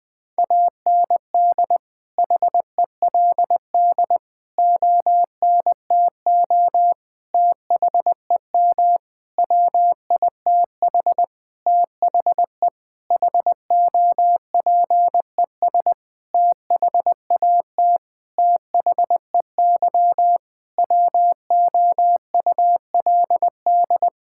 Morse code, heard as a keyed tone, about 20 wpm, 700 Hz. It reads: AND HELD ONTO THEM WITH THE HOPES THAT THEY WOULD
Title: Sentences with 9-Word Phrases - 20wpm